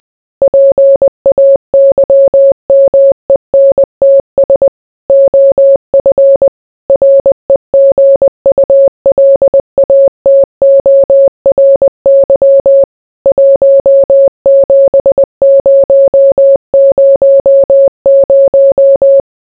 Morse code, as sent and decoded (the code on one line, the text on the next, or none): .--. .- -.-- -- . -. - ... / --- ..-. / .-. --. ..- .-.. .- - --- .-. -.-- / .---- --... ----- ----- -----
PAYMENTS OF REGULATORY 17000